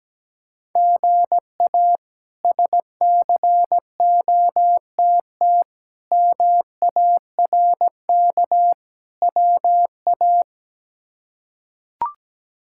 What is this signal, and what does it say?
Morse code, heard as a keyed tone, about 17 wpm, 700 Hz. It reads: GA SCOTT MARK WA
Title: K1USN SST QSO Practice - 17wpm